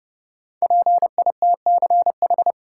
Morse code, heard as a keyed tone, about 32 wpm, 700 Hz. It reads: PITCH
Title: Sentences from Top 1300 Words - New Word Drill - 30wpm